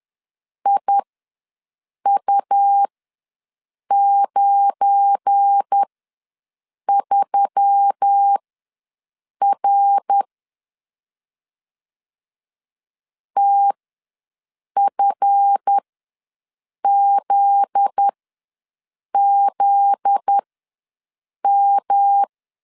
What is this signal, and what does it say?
Morse code, heard as a keyed tone, about 10 wpm, spaced as for 6 wpm, 800 Hz. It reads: IU93R TFZZM
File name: LEZIONI / 7 GF9Z / 7 GF9Z ws 10wpm 3919